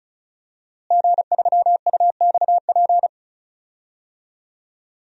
Morse code, heard as a keyed tone, about 34 wpm, 700 Hz. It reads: G3UXP
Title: DXCC-100 Call Signs - 5 Characters - Repeated 2x - 35wpm